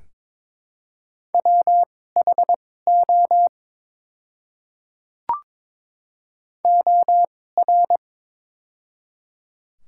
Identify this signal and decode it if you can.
Morse code, heard as a keyed tone, about 22 wpm, spaced as for 16 wpm, 700 Hz. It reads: WHO OR
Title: 2x Character Spacing - Top 100 Words - 22wpm